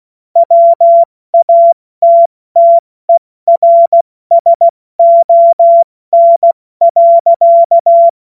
Morse code, heard as a keyed tone, about 16 wpm, spaced as for 14 wpm, 680 Hz. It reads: WATTERSON.
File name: Morsekurs / Witze / Witz 23 Januar 2024 70 F80